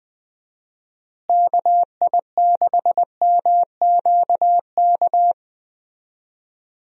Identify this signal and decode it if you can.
Morse code, heard as a keyed tone, about 21 wpm, 700 Hz. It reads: KI6MQK
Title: USA Call Signs - 6 Characters - Repeated 3x - 20wpm